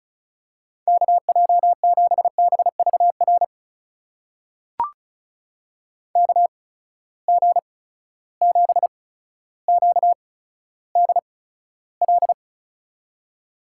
Morse code, heard as a keyed tone, about 36 wpm, spaced as for 15 wpm, 700 Hz. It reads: KJ7BVR KG7QDL
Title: USA Call Signs - 6 Characters - 8x Character Spacing - 35wpm